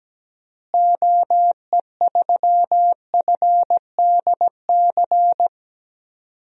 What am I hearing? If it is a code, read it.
Morse code, heard as a keyed tone, about 17 wpm, 700 Hz. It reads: OE3FDC
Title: DXCC-100 Call Signs - 6 Characters - Repeated 2x - 17wpm